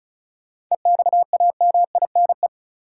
Morse code, heard as a keyed tone, about 34 wpm, 700 Hz. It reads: EXAMINE
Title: Sentences from Top 1600 Words - New Word Drill - 35wpm